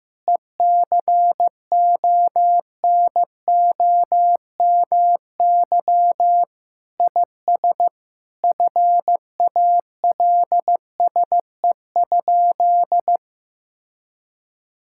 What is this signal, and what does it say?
Morse code, heard as a keyed tone, about 15 wpm, 700 Hz. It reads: ECONOMY IS FALSE?
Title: Sentences from Top 1600 Words 15wpm